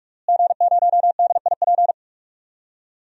Morse code, heard as a keyed tone, about 42 wpm, 700 Hz. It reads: G0DIP
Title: DXCC-100 Call Signs - 5 Characters - Repeated 2x - 45wpm